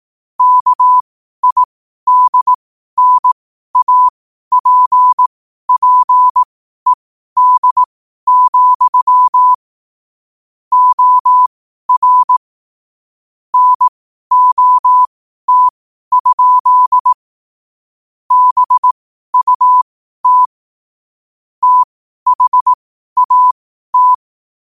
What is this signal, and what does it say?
Morse code, heard as a keyed tone, about 18 wpm, spaced as for 12 wpm, 1000 Hz. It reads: KIDNAPPED, OR NOT? BUT THAT
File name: qotdfiles / 12wpm